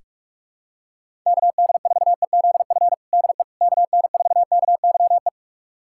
Morse code, heard as a keyed tone, about 41 wpm, 700 Hz. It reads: KD4EZF DE KN4KYE
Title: USA Callsign DE Callsign 45wpm